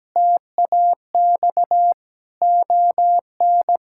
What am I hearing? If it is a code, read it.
Morse code, heard as a keyed tone, about 17 wpm, 700 Hz. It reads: TAX ON
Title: Sentences from Top 1000 Words - Review Part 5 - 17wpm